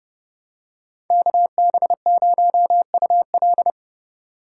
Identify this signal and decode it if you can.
Morse code, heard as a keyed tone, about 31 wpm, 700 Hz. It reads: KB0UL